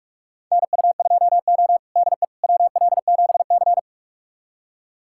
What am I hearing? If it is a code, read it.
Morse code, heard as a keyed tone, about 46 wpm, 700 Hz. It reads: NU2O DE WL7C
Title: USA Callsign DE Callsign - 4 Characters - 45wpm